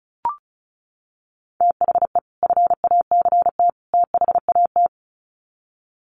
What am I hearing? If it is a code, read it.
Morse code, heard as a keyed tone, about 34 wpm, 700 Hz. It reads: THE FACT THAT
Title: Sets of 3 Words 35wpm